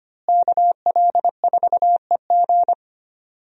Morse code, heard as a keyed tone, about 25 wpm, 700 Hz. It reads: KL4EG
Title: USA Call Signs - 5 Characters - Repeated 2x - 25wpm